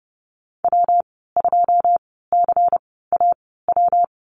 Morse code, heard as a keyed tone, about 30 wpm, spaced as for 17 wpm, 700 Hz. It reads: W2CAW
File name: Lesson 19g - Letter w - Call Signs 30wpm